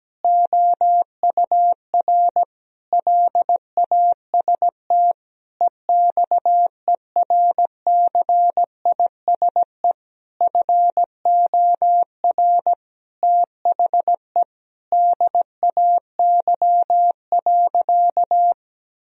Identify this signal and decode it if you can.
Morse code, heard as a keyed tone, about 17 wpm, 700 Hz. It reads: OUR LAST EXERCISE FOR THE DAY.
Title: Sentences from Top 1000 Words - Review Part 4 - 17wpm